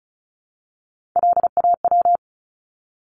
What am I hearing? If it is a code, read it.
Morse code, heard as a keyed tone, about 34 wpm, 700 Hz. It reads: LAW
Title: Top 501-750 Words 35wpm